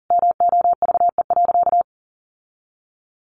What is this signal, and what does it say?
Morse code, heard as a keyed tone, about 40 wpm, 700 Hz. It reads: MOVE.
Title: Sentences from Top 1200 Words 40wpm